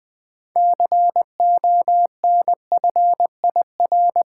Morse code, heard as a keyed tone, about 20 wpm, 700 Hz. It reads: CONFIR
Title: Build A Word - 6 Letters At A Time - 12-Letter Words - 20wpm